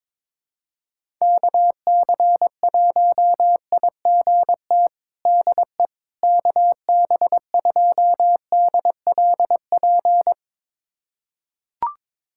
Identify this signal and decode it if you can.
Morse code, heard as a keyed tone, about 22 wpm, 700 Hz. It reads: KC1IGT DE KB2DLP
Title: USA Callsign DE Callsign - 6 Characters - 22wpm